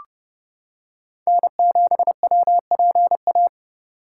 Morse code, heard as a keyed tone, about 29 wpm, 700 Hz. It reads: N7WPA